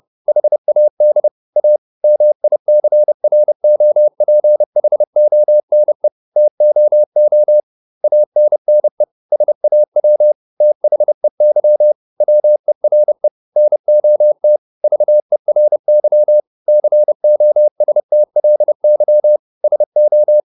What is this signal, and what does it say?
Morse code, heard as a keyed tone, about 30 wpm, 600 Hz. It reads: HAD A MICROPHONE TOO ANNE SAW THEY WERE NOT VERY COSTLY SO